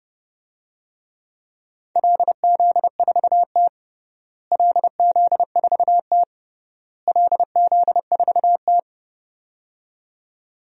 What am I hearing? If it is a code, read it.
Morse code, heard as a keyed tone, about 30 wpm, 700 Hz. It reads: LZ4T LZ4T LZ4T
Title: DXCC-100 Call Signs - 4 Characters - Repeated 3x - 30wpm